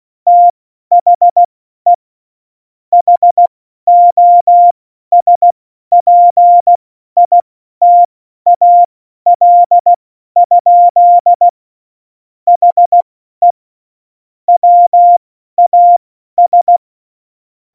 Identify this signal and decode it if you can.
Morse code, heard as a keyed tone, about 16 wpm, spaced as for 12 wpm, 700 Hz. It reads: THE HOSPITAL? HE WAS
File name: Morsekurs / Witze